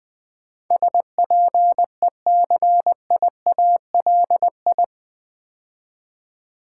Morse code, heard as a keyed tone, about 20 wpm, 700 Hz. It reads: SPECIALI
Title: Build A Word - 4 Letters At A Time - 11-Letter Words - 20wpm